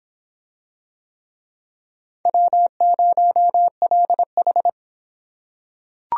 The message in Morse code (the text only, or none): W0LH